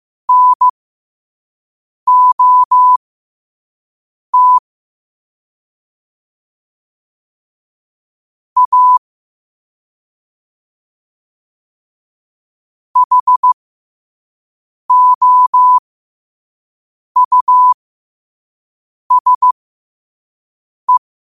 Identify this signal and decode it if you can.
Morse code, heard as a keyed tone, about 15 wpm, spaced as for 5 wpm, 1000 Hz. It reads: NOT A HOUSE
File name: qotdfiles / 5wpm